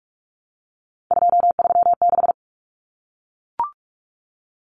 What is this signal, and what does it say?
Morse code, heard as a keyed tone, about 44 wpm, 700 Hz. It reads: 236
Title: Numbers - 3-Digit 45wpm